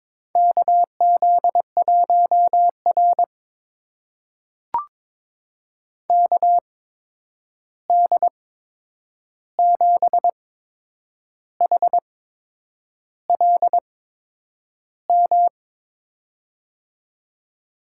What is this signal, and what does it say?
Morse code, heard as a keyed tone, about 22 wpm, spaced as for 8 wpm, 700 Hz. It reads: KZ1R KD7HLM
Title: USA Call Signs - 8x Character Spacing - 22wpm